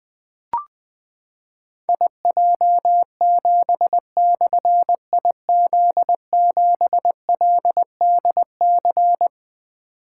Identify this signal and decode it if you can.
Morse code, heard as a keyed tone, about 20 wpm, 700 Hz. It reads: IJ7/IZ7LDC
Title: Difficult Call Signs 20wpm